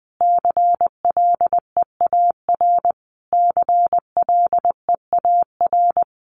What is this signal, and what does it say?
Morse code, heard as a keyed tone, about 20 wpm, 700 Hz. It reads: CLEAR CLEAR
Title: Top 500 Words Repeated 20wpm